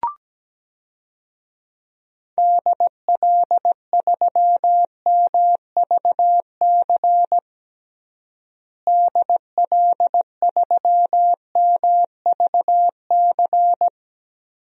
Morse code, heard as a keyed tone, about 17 wpm, 700 Hz. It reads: DL3MVC DL3MVC
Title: DXCC-100 Call Signs - 6 Characters - Repeated 3x - 17wpm